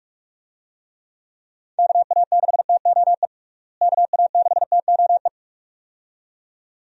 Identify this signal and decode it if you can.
Morse code, heard as a keyed tone, about 46 wpm, 700 Hz. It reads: KA6TOE KA6TOE
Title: USA Call Signs - 6 Characters - Repeated 2x - 45wpm